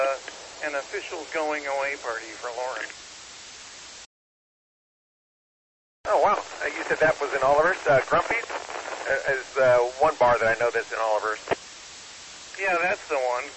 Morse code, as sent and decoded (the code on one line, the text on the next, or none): none